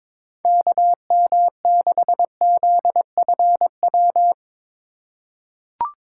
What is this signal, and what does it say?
Morse code, heard as a keyed tone, about 21 wpm, 700 Hz. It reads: KM6ZFW
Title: USA Call Signs - 6 Characters - 4x Character Spacing - 22wpm